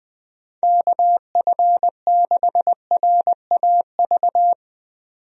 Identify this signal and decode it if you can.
Morse code, heard as a keyed tone, about 20 wpm, 700 Hz. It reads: KF6RAV
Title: USA Call Signs - 6 Characters - 20wpm